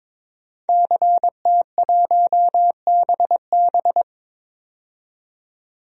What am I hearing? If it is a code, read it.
Morse code, heard as a keyed tone, about 22 wpm, 700 Hz. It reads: CT1BB
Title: DXCC-100 Call Signs - 5 Characters - 22wpm